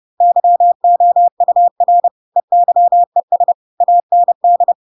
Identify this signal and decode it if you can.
Morse code, heard as a keyed tone, about 30 wpm, 700 Hz. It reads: YOUR EYES AND
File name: Pinocchio / Pinocchio ch8 30 30 ik1hge